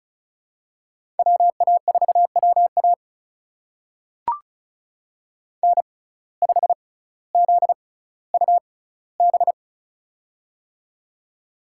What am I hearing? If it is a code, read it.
Morse code, heard as a keyed tone, about 36 wpm, spaced as for 18 wpm, 700 Hz. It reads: WA4WA N5ZUB
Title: USA Call Signs - 6x Character Spacing - 35wpm